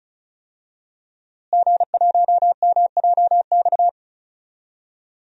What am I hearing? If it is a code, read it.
Morse code, heard as a keyed tone, about 36 wpm, 700 Hz. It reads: G1MJX
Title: DXCC-100 Call Signs - 5 Characters - Repeated 3x - 35wpm